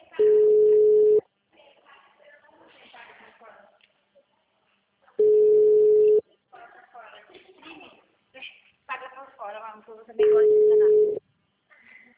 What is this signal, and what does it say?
Ringback tone (Europe, 425 Hz), with speech.
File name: out- -2024- - - .95552